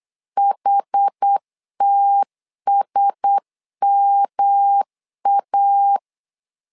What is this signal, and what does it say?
Morse code, heard as a keyed tone, about 8 wpm, 800 Hz. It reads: HTSMA
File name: LEZIONI / 4 Q1S 8wpm 2710